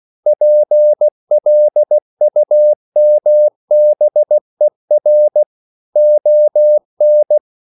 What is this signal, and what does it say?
Morse code, heard as a keyed tone, about 16 wpm, 600 Hz. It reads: PLUMBER ON